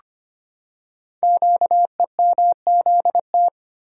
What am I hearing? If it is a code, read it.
Morse code, heard as a keyed tone, about 25 wpm, 700 Hz. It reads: QEMZT
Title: Code Groups - 5 Letters - 25wpm